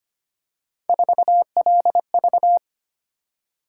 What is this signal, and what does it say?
Morse code, heard as a keyed tone, about 25 wpm, 700 Hz. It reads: 4LV
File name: Three Letter-Number Combinations - 25wpm